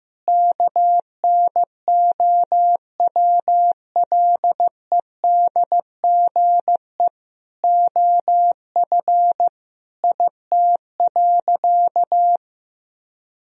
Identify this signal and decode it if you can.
Morse code, heard as a keyed tone, about 16 wpm, 700 Hz. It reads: KNOWLEDGE OF IT.
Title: Sentences from Top 1000 Words - Review Part 3 - 15wpm